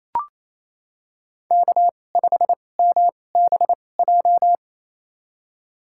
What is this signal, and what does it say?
Morse code, heard as a keyed tone, about 28 wpm, spaced as for 20 wpm, 700 Hz. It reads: K5MBJ
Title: USA Call Signs - 5 Characters - 2x Character Spacing - 28wpm